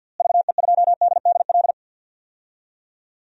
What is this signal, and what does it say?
Morse code, heard as a keyed tone, about 52 wpm, 700 Hz. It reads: VE2DDL
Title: DXCC-100 Call Signs - 6 Characters - Repeated 2x - 50wpm